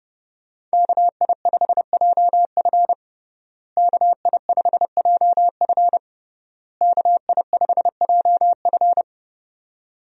Morse code, heard as a keyed tone, about 30 wpm, 700 Hz. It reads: KI5JF KI5JF KI5JF
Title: USA Call Signs - 5 Characters - Repeated 3x - 30wpm